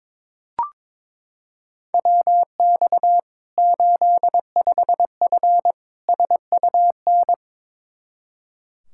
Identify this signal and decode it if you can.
Morse code, heard as a keyed tone, about 22 wpm, 700 Hz. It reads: WX 85F SUN
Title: Practice QSO - Exchange 2 - Info - 22wpm